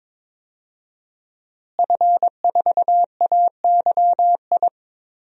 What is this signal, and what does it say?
Morse code, heard as a keyed tone, about 22 wpm, 700 Hz. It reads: F4AYI
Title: DXCC-100 Call Signs - 5 Characters - Repeated 2x - 22wpm